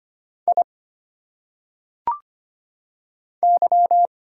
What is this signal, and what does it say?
Morse code, heard as a keyed tone, about 25 wpm, 700 Hz. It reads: I Y
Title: Single Letters - Phonetic Pronunciation 25wpm